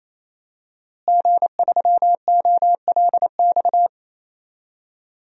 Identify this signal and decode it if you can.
Morse code, heard as a keyed tone, about 27 wpm, 700 Hz. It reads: G3OLX
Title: DXCC-100 Call Signs - 5 Characters - Repeated 3x - 28wpm